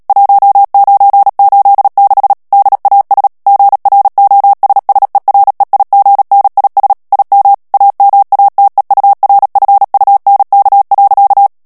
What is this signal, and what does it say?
Morse code, heard as a keyed tone, about 37 wpm, 800 Hz. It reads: 1986 DAS GROSSEREIGNIS IM AMATEURFUNK.
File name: morse mp3 / ALLEMAND texte clair 2 vitesse 35 mots minute